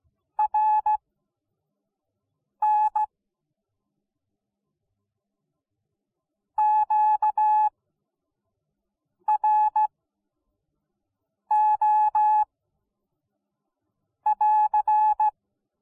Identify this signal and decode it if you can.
Morse code, heard as a keyed tone, about 17 wpm, spaced as for 4 wpm, 850 Hz. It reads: RNQRO <AR>